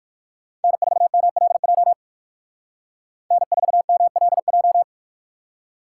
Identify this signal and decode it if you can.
Morse code, heard as a keyed tone, about 61 wpm, 700 Hz. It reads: N4MLJ N4MLJ